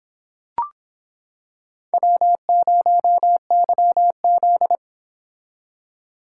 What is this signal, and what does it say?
Morse code, heard as a keyed tone, about 26 wpm, 700 Hz. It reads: W0YZ